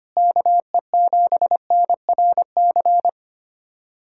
Morse code, heard as a keyed tone, about 25 wpm, 700 Hz. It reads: KE7NRC
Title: USA Callsign DE Callsign - 6 Characters - 25wpm